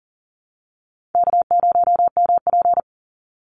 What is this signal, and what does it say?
Morse code, heard as a keyed tone, about 40 wpm, 700 Hz. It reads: K0MP
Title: Call Signs Repeated 3x - 40wpm